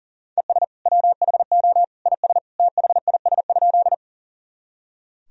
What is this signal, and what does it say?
Morse code, heard as a keyed tone, about 40 wpm, 700 Hz. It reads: ES WHO IS THIS?
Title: Sentences from Top 100 Words and Abbreviations 40wpm